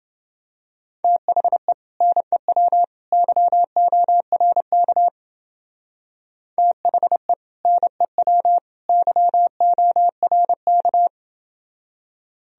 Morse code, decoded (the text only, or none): THE NEW YORK THE NEW YORK